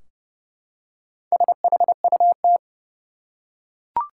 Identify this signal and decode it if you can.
Morse code, heard as a keyed tone, about 30 wpm, 700 Hz. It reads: SHUT
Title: Four Letter Words - 30wpm